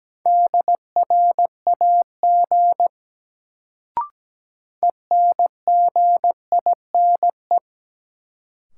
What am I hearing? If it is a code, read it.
Morse code, heard as a keyed tone, about 17 wpm, 700 Hz. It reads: DRAG ENGINE
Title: Sentences from Top 1500 Words - New Word Drill Alt - 17wpm